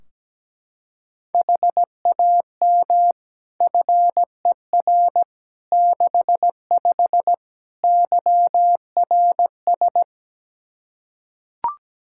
Morse code, heard as a keyed tone, about 17 wpm, 700 Hz. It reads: HAM FER 65 YRS
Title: K1USN SST QSO Practice 17wpm